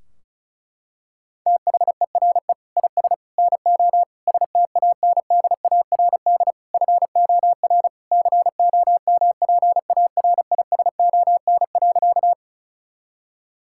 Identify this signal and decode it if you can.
Morse code, heard as a keyed tone, about 36 wpm, 700 Hz. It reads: THERE IS NO STANDARD FOR COMPARISON.
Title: Sentences from Top 1600 Words 35wpm